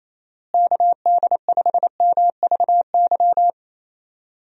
Morse code, heard as a keyed tone, about 28 wpm, 700 Hz. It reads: KD5MVY
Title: USA Call Signs - 6 Characters - Repeated 2x - 28wpm